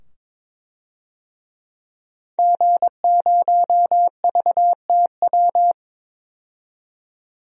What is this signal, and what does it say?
Morse code, heard as a keyed tone, about 21 wpm, 700 Hz. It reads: G0VTW
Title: DXCC-100 Call Signs - 5 Characters - 22wpm